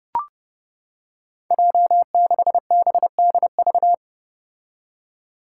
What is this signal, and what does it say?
Morse code, heard as a keyed tone, about 30 wpm, 700 Hz. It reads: J6BDV
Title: Five Challenging Letter-Number Combinations 30wpm